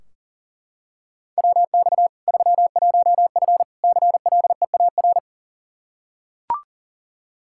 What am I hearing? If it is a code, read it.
Morse code, heard as a keyed tone, about 40 wpm, 700 Hz. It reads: WX 31F CLEAR
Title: Practice QSO - Exchange 2 - Info - 40wpm